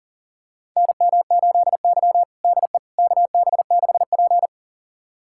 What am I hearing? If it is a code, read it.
Morse code, heard as a keyed tone, about 40 wpm, 700 Hz. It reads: NM8Y DE KB6P
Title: USA Callsign DE Callsign - 4 Characters - 40wpm